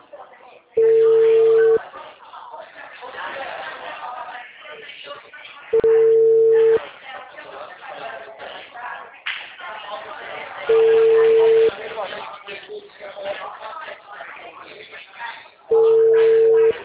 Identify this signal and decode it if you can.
Ringback tone (China, 450 Hz), with speech.
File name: out- -2016- - - .5568